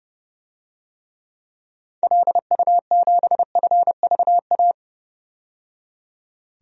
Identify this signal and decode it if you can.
Morse code, heard as a keyed tone, about 30 wpm, 700 Hz. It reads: LU7FVA
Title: DXCC-100 Call Signs - 6 Characters - 30wpm